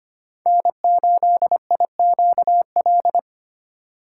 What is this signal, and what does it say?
Morse code, heard as a keyed tone, about 25 wpm, 700 Hz. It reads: N8IQL
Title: USA Call Signs - 2x Character Spacing - 25wpm